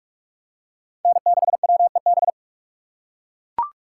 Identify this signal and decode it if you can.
Morse code, heard as a keyed tone, about 42 wpm, 700 Hz. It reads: N6WEB